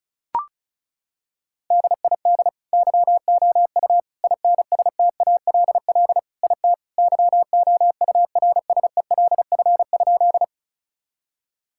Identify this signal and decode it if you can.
Morse code, heard as a keyed tone, about 34 wpm, 700 Hz. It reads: DID YOU INSTALL IT YOURSELF?
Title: Sentences from Top 1600 Words 35wpm